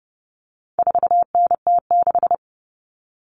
Morse code, heard as a keyed tone, about 30 wpm, 700 Hz. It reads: 4NT6
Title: All Cut Numbers - 4-Digit 30wpm